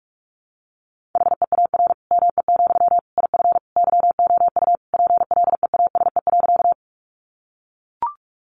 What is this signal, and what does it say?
Morse code, heard as a keyed tone, about 44 wpm, 700 Hz. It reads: HEAR ME, IF YOU PLEASE.
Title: Sentences from Top 1500 Words 45wpm